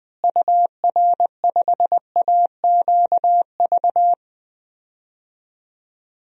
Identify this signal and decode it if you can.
Morse code, heard as a keyed tone, about 20 wpm, 700 Hz. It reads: UR5AQV
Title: DXCC-100 Call Signs - 6 Characters - Repeated 2x - 20wpm